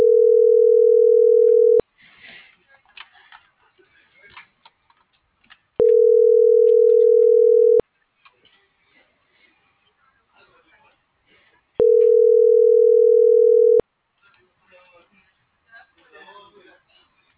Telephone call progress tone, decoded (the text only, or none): ringback tone